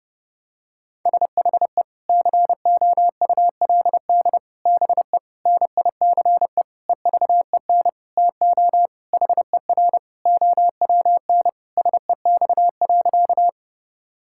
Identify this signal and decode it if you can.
Morse code, heard as a keyed tone, about 32 wpm, 700 Hz. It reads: SHE COULD BE NICE EVEN TO HER OWN SEX.
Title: Sentences from Top 1600 Words 30wpm